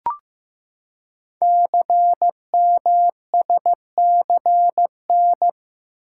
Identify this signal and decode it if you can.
Morse code, heard as a keyed tone, about 15 wpm, 700 Hz. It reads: CMSCN
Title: Code Groups - 5 Letters - 15wpm